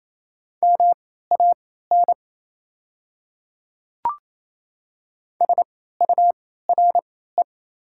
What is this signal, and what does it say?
Morse code, heard as a keyed tone, about 28 wpm, spaced as for 16 wpm, 700 Hz. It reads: MAN SURE
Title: 3x Character Spacing - Top 300 Words - 28wpm